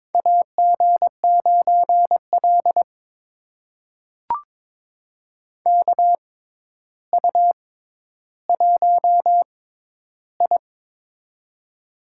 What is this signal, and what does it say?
Morse code, heard as a keyed tone, about 22 wpm, spaced as for 11 wpm, 700 Hz. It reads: AG9L KU1I
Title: USA Call Signs - 4 Characters - 6x Character Spacing - 22wpm